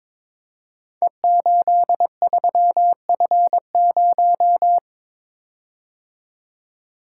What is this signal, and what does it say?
Morse code, heard as a keyed tone, about 22 wpm, 700 Hz. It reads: E83F0
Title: Code Groups - 5 Letter:Numbers - 22wpm